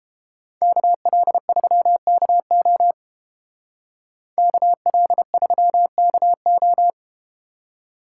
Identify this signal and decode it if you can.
Morse code, heard as a keyed tone, about 31 wpm, 700 Hz. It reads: KL3KO KL3KO